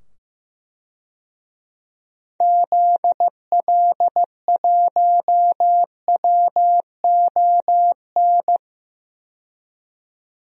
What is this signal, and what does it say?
Morse code, heard as a keyed tone, about 15 wpm, 700 Hz. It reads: ZL1WON
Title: DXCC-100 Call Signs - 6 Characters - 15wpm